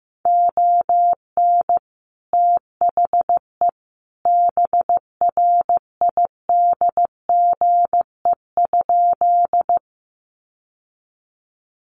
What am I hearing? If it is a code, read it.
Morse code, heard as a keyed tone, about 15 wpm, 700 Hz. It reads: ON THE BRIDGE?
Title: Sentences from Top 1200 Words 15wpm